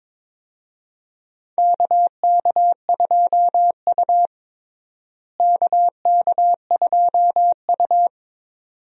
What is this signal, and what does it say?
Morse code, heard as a keyed tone, about 22 wpm, 700 Hz. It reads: KK2U KK2U
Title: USA Call Signs - 4 Characters - Repeated 2x - 22wpm